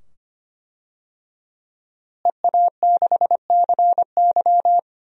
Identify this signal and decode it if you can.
Morse code, heard as a keyed tone, about 25 wpm, 700 Hz. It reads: EA6CY
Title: DXCC-100 Call Signs - 5 Characters - Repeated 2x - 25wpm